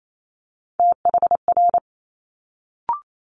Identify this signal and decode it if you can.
Morse code, heard as a keyed tone, about 28 wpm, 700 Hz. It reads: THR
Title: Multi-letter Phonemes Quick Fire 28wpm